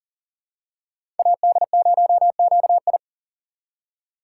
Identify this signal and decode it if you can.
Morse code, heard as a keyed tone, about 40 wpm, 700 Hz. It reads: AD0QI
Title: USA Call Signs - 5 Characters - 2x Character Spacing - 40wpm